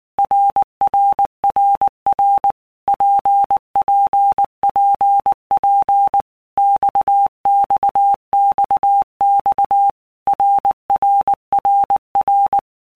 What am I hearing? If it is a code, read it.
Morse code, heard as a keyed tone, about 19 wpm, 800 Hz. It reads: RRRR PPPP XXXX RRRR
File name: All Groups 1-minute Blocks / Groups@18 / RKPX